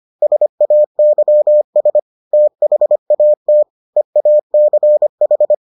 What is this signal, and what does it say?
Morse code, heard as a keyed tone, about 25 wpm, 600 Hz. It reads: SAYS THAT EACH